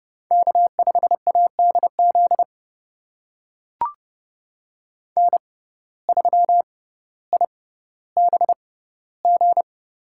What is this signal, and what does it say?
Morse code, heard as a keyed tone, about 29 wpm, spaced as for 15 wpm, 700 Hz. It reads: K5ADZ N3IBG
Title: USA Call Signs - 5 Characters - 6x Character Spacing - 30wpm